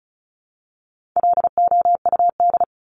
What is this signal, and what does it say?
Morse code, heard as a keyed tone, about 34 wpm, 700 Hz. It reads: LOUD
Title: Top 1751-2000 Words 35wpm